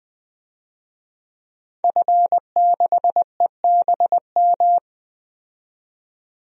Morse code, heard as a keyed tone, about 20 wpm, 700 Hz. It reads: F6EBM
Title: DXCC-100 Call Signs - 5 Characters - 20wpm